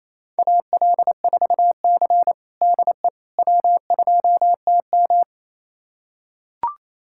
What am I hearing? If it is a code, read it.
Morse code, heard as a keyed tone, about 28 wpm, 700 Hz. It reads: AL4C DE W2TM
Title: USA Callsign DE Callsign - 4 Characters - 28wpm